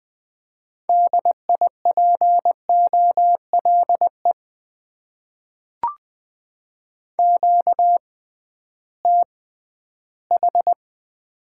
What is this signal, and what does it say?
Morse code, heard as a keyed tone, about 20 wpm, spaced as for 13 wpm, 700 Hz. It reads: DIPOLE QTH
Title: Top 90 QSO Elements - 6x Character Spacing - 20wpm